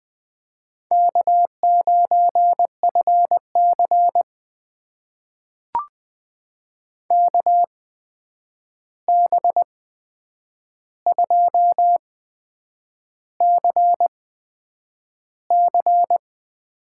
Morse code, heard as a keyed tone, about 20 wpm, spaced as for 8 wpm, 700 Hz. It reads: K9FC KB2CC
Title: USA Call Signs - 8x Character Spacing - 20wpm